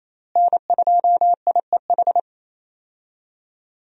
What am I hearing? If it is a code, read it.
Morse code, heard as a keyed tone, about 27 wpm, 700 Hz. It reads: N2IEH
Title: USA Call Signs - 5 Characters - Repeated 3x - 28wpm